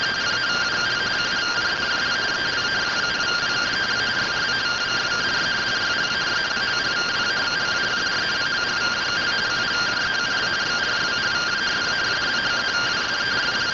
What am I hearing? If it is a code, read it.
Two-tone teleprinter signal, about 170 Hz shift.